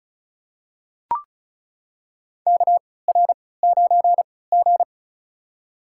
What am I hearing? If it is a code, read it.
Morse code, heard as a keyed tone, about 34 wpm, spaced as for 20 wpm, 700 Hz. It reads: KR9G
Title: USA Call Signs - 4 Characters - 3x Character Spacing - 35wpm